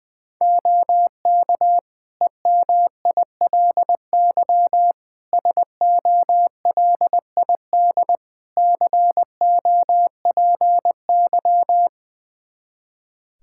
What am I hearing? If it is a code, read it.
Morse code, heard as a keyed tone, about 20 wpm, 700 Hz. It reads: OK EMILY SOLID COPY
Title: Practice QSO - Exchange 3 - More Info - 20wpm